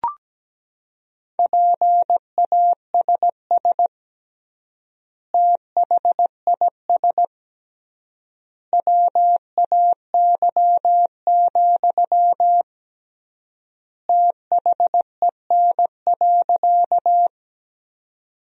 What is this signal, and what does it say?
Morse code, heard as a keyed tone, about 17 wpm, 700 Hz. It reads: PASS THIS WAY, THEN.
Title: 3x Word-Spacing - Sentences from Top 400 Words - 17wpm